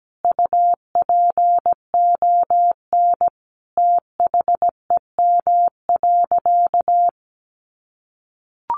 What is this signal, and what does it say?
Morse code, heard as a keyed tone, about 17 wpm, 700 Hz. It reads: UPON THEM.
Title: Sentences from Top 1400 Words 17wpm